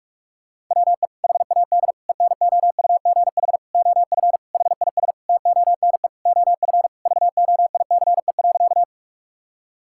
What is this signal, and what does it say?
Morse code, heard as a keyed tone, about 46 wpm, 700 Hz. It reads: WE HAD ENOUGH OF HIS TONE OF VOICE.
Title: Sentences from Top 1600 Words 45wpm